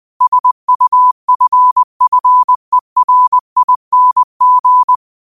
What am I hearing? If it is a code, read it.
Morse code, heard as a keyed tone, about 20 wpm, 1000 Hz. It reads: SUFFERING
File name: qotdfiles / 20wpm